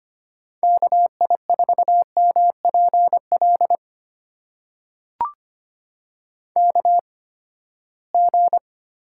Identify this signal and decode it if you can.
Morse code, heard as a keyed tone, about 25 wpm, spaced as for 17 wpm, 700 Hz. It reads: KI4MPL KG
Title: USA Call Signs - 6 Characters - 8x Character Spacing - 25wpm